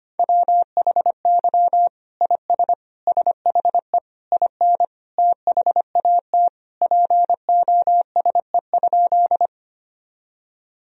Morse code, heard as a keyed tone, about 25 wpm, 700 Hz. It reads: WHY IS SHE IN THAT POSE?
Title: Sentences from Top 1000 Words - Review Part 2 - 25wpm